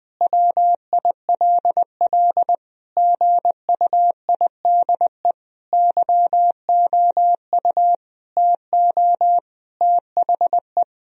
Morse code, heard as a keyed tone, about 21 wpm, 700 Hz. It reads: WILL GUIDE YOU TO THE
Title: Sentences from Top 1000 Words - Review Part 3 - 20wpm